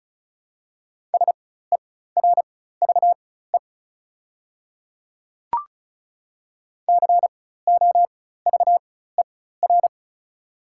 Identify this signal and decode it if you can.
Morse code, heard as a keyed tone, about 36 wpm, spaced as for 16 wpm, 700 Hz. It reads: SERVE COVER
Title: 4x Character Spacing - Top 500 Words - 35wpm